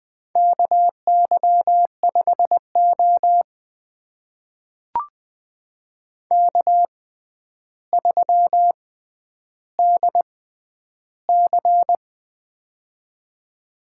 Morse code, heard as a keyed tone, about 20 wpm, spaced as for 10 wpm, 700 Hz. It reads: KY5O K3DC
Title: USA Call Signs - 4 Characters - 6x Character Spacing - 20wpm